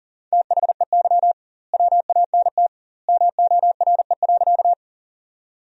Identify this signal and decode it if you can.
Morse code, heard as a keyed tone, about 40 wpm, 700 Hz. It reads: THEY WANT MORE.
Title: 2x Word-Spacing - Sentences from Top 200 Words - 40wpm